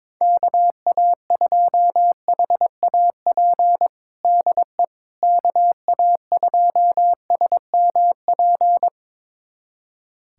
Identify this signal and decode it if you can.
Morse code, heard as a keyed tone, about 22 wpm, 700 Hz. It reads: KA2HAP DE KA2SMP
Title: USA Callsign DE Callsign - 6 Characters - 22wpm